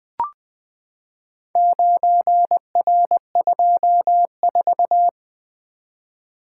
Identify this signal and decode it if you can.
Morse code, heard as a keyed tone, about 20 wpm, 700 Hz. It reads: 9R24
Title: Four Letter-Number Combinations - 20wpm